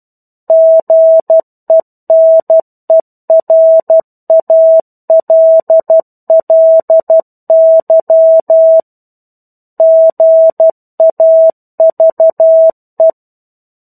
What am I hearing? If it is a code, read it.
Morse code, heard as a keyed tone, about 12 wpm, 650 Hz. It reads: GENERALLY GAVE
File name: 01-Chapter-01 12 WPM